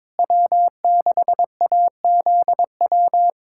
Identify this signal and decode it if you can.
Morse code, heard as a keyed tone, about 22 wpm, 700 Hz. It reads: W6AZW
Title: USA Callsign DE Callsign - 5 Characters - 22wpm